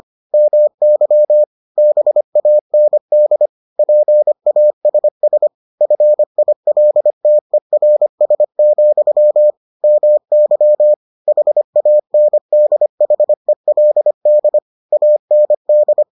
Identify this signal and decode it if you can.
Morse code, heard as a keyed tone, about 25 wpm, 600 Hz. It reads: MY BAND PASS FILTERS, MY HANDHELD AND